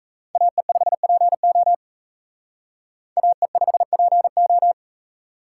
Morse code, heard as a keyed tone, about 38 wpm, 700 Hz. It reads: AE5PO AE5PO